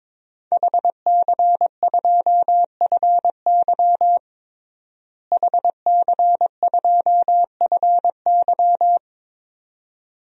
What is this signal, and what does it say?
Morse code, heard as a keyed tone, about 22 wpm, 700 Hz. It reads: HC2FY HC2FY
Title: DXCC-100 Call Signs - 5 Characters - Repeated 2x - 22wpm